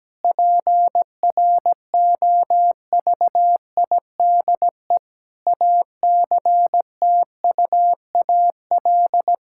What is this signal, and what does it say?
Morse code, heard as a keyed tone, about 17 wpm, 700 Hz. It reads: PROVIDE ACTUAL
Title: Sentences from Top 1000 Words - Review Part 5 - 17wpm